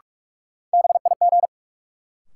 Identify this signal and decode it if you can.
Morse code, heard as a keyed tone, about 46 wpm, 700 Hz. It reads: BIG
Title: Sets of 2 Words - New Word Drill - 45wpm